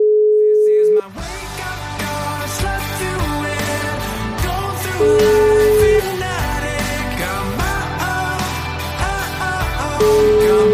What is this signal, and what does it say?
Ringback tone (Europe, 425 Hz), with noise.